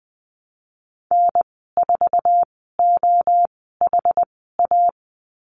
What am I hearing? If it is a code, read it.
Morse code, heard as a keyed tone, about 20 wpm, spaced as for 15 wpm, 700 Hz. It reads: N4OHA
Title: 062 Lesson 11g - Letter h - Call Signs 20wpm